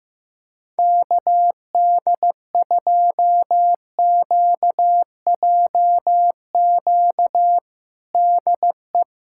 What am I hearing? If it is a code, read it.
Morse code, heard as a keyed tone, about 16 wpm, 700 Hz. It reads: KD2QJQ DE
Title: USA Callsign DE Callsign 15wpm